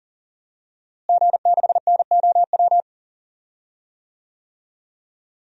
Morse code, heard as a keyed tone, about 40 wpm, 700 Hz. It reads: G6NOW
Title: DXCC-100 Call Signs - 5 Characters - Repeated 3x - 40wpm